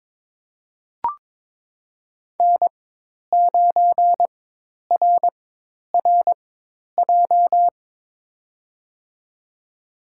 Morse code, heard as a keyed tone, about 21 wpm, spaced as for 10 wpm, 700 Hz. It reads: N9RRJ
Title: USA Call Signs - 5 Characters - 4x Character Spacing - 22wpm